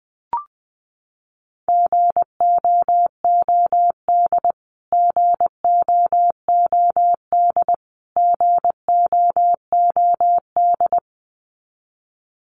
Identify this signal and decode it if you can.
Morse code, heard as a keyed tone, about 20 wpm, 700 Hz. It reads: GOOD GOOD GOOD
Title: Top 300 Words Repeated 20wpm